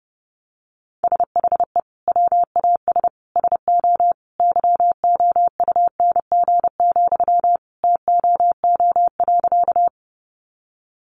Morse code, heard as a keyed tone, about 30 wpm, 700 Hz. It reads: SHE WAS SO YOUNG, TOO.
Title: Sentences from Top 400 Words 30wpm